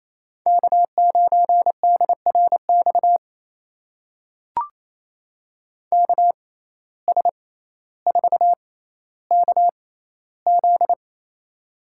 Morse code, heard as a keyed tone, about 28 wpm, spaced as for 14 wpm, 700 Hz. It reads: K9DRX KS4KZ
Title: USA Call Signs - 6x Character Spacing - 28wpm